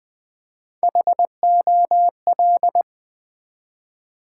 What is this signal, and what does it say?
Morse code, heard as a keyed tone, about 20 wpm, 700 Hz. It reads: HOL
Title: Build A Word - 1 Letter At A Time - 3 to 5-Letter Words - 20wpm